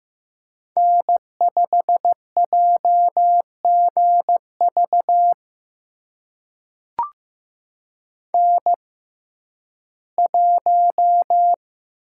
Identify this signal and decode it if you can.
Morse code, heard as a keyed tone, about 15 wpm, spaced as for 11 wpm, 700 Hz. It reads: N5JGV N1
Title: USA Call Signs - 6x Character Spacing - 15wpm